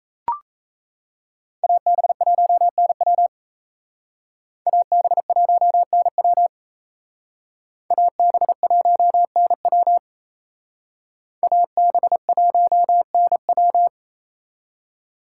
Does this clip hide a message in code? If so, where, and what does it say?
Yes: Morse code from 1.6 to 13.9 s, heard as a keyed tone, about 35 wpm, 700 Hz. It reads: AB1NW AB1NW AB1NW AB1NW